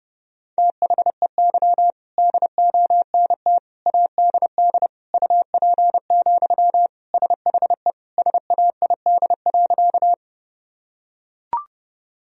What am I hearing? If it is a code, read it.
Morse code, heard as a keyed tone, about 30 wpm, 700 Hz. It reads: THEY DONT ADD UP, SHE SAID.
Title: Sentences from Top 1000 Words - Review Part 1 - 30wpm